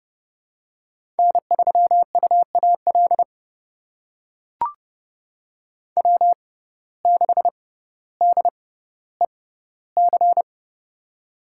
Morse code, heard as a keyed tone, about 30 wpm, spaced as for 15 wpm, 700 Hz. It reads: N3UAL W6DEC
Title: USA Call Signs - 6x Character Spacing - 30wpm